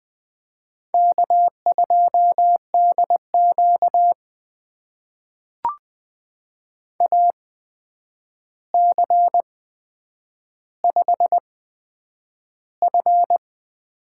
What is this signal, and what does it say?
Morse code, heard as a keyed tone, about 20 wpm, spaced as for 9 wpm, 700 Hz. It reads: K2DQ AC5F
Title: USA Call Signs - 4 Characters - 8x Character Spacing - 20wpm